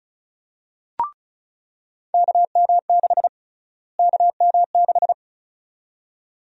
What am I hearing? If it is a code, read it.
Morse code, heard as a keyed tone, about 36 wpm, 700 Hz. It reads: KM6 KM6
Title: USA Call Sign Prefixes - Repeated 2x - 35wpm